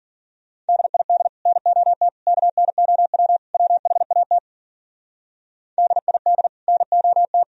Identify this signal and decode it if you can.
Morse code, heard as a keyed tone, about 51 wpm, 700 Hz. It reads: DID NOT KNOW WHAT DID NOT